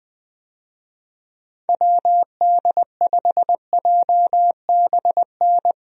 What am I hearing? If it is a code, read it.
Morse code, heard as a keyed tone, about 20 wpm, 700 Hz. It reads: WD5JBN
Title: USA Call Signs - 6 Characters - Repeated 2x - 20wpm